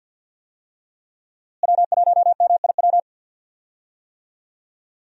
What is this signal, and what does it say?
Morse code, heard as a keyed tone, about 52 wpm, 690 Hz. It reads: W1MIW
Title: USA Call Signs - 5 Characters - Repeated 3x - 50wpm